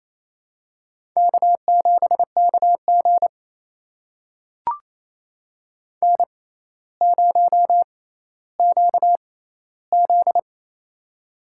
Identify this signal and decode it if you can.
Morse code, heard as a keyed tone, about 28 wpm, spaced as for 14 wpm, 700 Hz. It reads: K7KG N0QZ
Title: USA Call Signs - 4 Characters - 6x Character Spacing - 28wpm